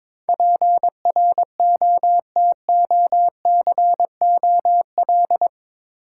Morse code, heard as a keyed tone, about 22 wpm, 700 Hz. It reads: PROTOCOL
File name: practice / 22wpm / Eight Letter Words - 22wpm